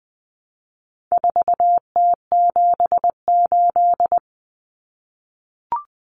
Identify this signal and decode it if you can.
Morse code, heard as a keyed tone, about 20 wpm, 700 Hz. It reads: 4T78
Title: Common Cut Numbers - 4-Digit 20wpm